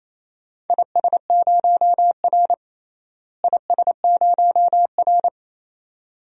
Morse code, heard as a keyed tone, about 28 wpm, 700 Hz. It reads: IS0R IS0R
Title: DXCC-100 Call Signs - 4 Characters - Repeated 2x - 28wpm